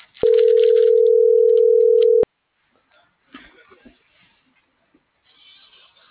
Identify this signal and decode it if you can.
Ringback tone (North America), 440 + 480 Hz.